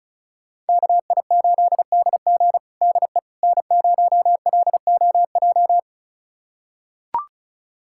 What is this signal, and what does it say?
Morse code, heard as a keyed tone, about 36 wpm, 700 Hz. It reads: KI8DG DE N0LOJ
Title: USA Callsign DE Callsign - 5 Characters - 35wpm